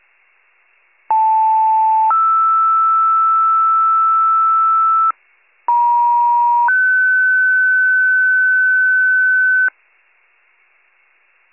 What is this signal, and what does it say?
Two-tone page, heard as a one-second tone, then a three-second tone, about 880 then 1320 Hz, about 950 then 1500 Hz. It is heard from 1.1 to 9.7 s.
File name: 57 FIRE